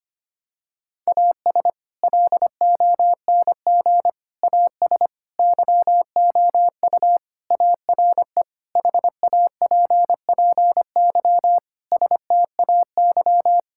Morse code, heard as a keyed tone, about 25 wpm, 700 Hz. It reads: AS LONG AS YOU ARE HAPPY STAY
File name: practice / 25wpm / Sentences with 4-Word Phrases - 25wpm